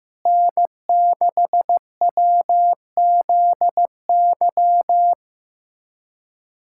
Morse code, heard as a keyed tone, about 15 wpm, 700 Hz. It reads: N6WZY
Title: USA Callsign DE Callsign - 5 Characters - 15wpm